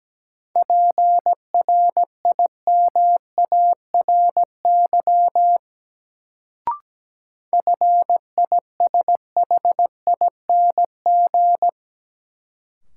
Morse code, heard as a keyed tone, about 17 wpm, 700 Hz. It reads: PRIMARY FISHING
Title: Sentences from Top 1000 Words - Review Part 5 - New Word Drill - 17wpm